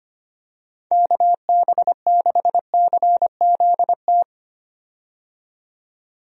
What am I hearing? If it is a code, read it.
Morse code, heard as a keyed tone, about 25 wpm, 700 Hz. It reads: KB6CZT
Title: USA Call Signs - 6 Characters - 25wpm